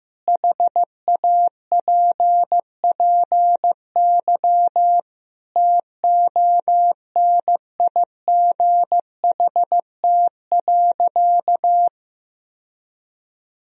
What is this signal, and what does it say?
Morse code, heard as a keyed tone, about 15 wpm, 700 Hz. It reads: HAPPY TONIGHT.
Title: Sentences from Top 1000 Words - Review Part 3 - 15wpm